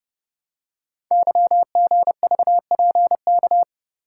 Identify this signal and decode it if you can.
Morse code, heard as a keyed tone, about 30 wpm, 700 Hz. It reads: YGVPK
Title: Code Groups - 5 Letters - 30wpm